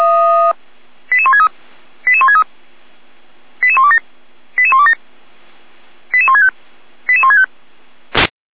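Five-tone selective call (ZVEI), heard as quick runs of five tones, tones of about 70 ms. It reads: 88163 88163 88127 88127 88165 88165
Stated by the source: ZVEI five-tone selective call 88303 received on an SDR.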